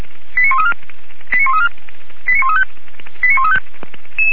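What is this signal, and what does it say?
Five-tone selective call (ZVEI), heard as quick runs of five tones, tones of about 80 ms. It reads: 89134 89134 89135 89135
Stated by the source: ZVEI five-tone selective call received on an SDR.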